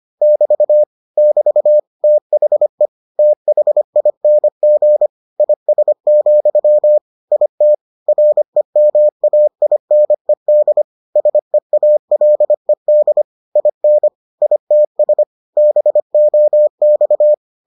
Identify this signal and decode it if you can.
Morse code, heard as a keyed tone, about 25 wpm, 600 Hz. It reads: <BT> <BT> THE THING IS, IT REMAINED SEALED IN ITS BOX